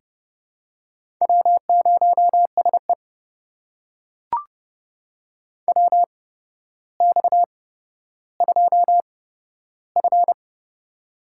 Morse code, heard as a keyed tone, about 30 wpm, spaced as for 13 wpm, 700 Hz. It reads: W0SE WX2F
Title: USA Call Signs - 4 Characters - 8x Character Spacing - 30wpm